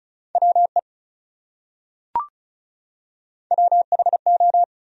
Morse code, heard as a keyed tone, about 34 wpm, 700 Hz. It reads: WE WHO
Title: Top 100 Words Repeated 35wpm